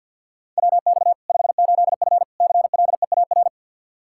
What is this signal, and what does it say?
Morse code, heard as a keyed tone, about 53 wpm, 700 Hz. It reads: WX 58F CLEAR
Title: Practice QSO - Exchange 2 - Info - 50wpm